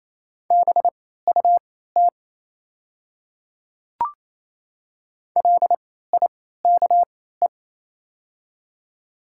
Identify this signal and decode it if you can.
Morse code, heard as a keyed tone, about 28 wpm, spaced as for 16 wpm, 690 Hz. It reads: BUT LIKE
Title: 3x Character Spacing - Top 100 Words - 28wpm